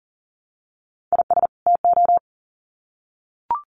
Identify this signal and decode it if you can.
Morse code, heard as a keyed tone, about 40 wpm, 700 Hz. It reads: IS TO
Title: Sets of 2 Words 40wpm